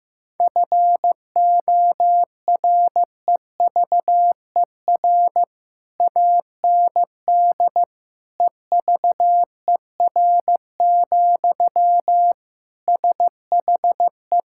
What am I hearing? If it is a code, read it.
Morse code, heard as a keyed tone, about 16 wpm, 700 Hz. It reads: FOREVER AND EVER, SHE
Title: Sentences from Top 1600 Words 15wpm